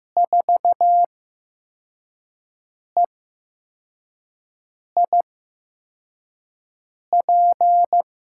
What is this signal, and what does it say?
Morse code, heard as a keyed tone, about 15 wpm, spaced as for 4 wpm, 700 Hz. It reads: 4EIP
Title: USA Call Signs - 5 Characters - 8x Character Spacing - 15wpm